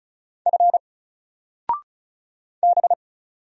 Morse code, heard as a keyed tone, about 35 wpm, 700 Hz. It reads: F B
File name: Single-Letter-Number-35wpm